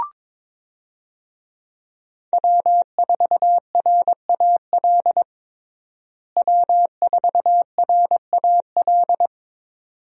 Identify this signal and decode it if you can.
Morse code, heard as a keyed tone, about 22 wpm, 700 Hz. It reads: W4RAL W4RAL
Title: USA Call Signs - 5 Characters - Repeated 2x - 22wpm